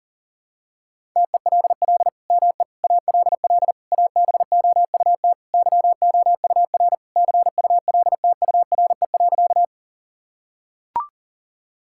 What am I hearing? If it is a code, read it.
Morse code, heard as a keyed tone, about 37 wpm, 700 Hz. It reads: TELL ME ALL ABOUT YOUR CULTURE.
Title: Sentences from Top 1000 Words - Review Part 6 - 40wpm